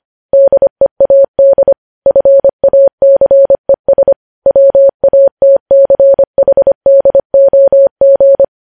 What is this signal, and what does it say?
Morse code, heard as a keyed tone, about 25 wpm, 550 Hz. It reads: DEAD FACES WATCHDOG
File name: podcast / News headlines-1x-25x25